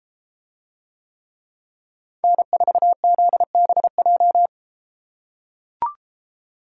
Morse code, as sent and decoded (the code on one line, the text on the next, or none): -. ....- --.. -... .---
N4ZBJ